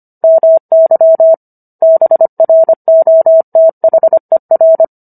Morse code, heard as a keyed tone, about 25 wpm, 650 Hz. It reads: MY BROTHER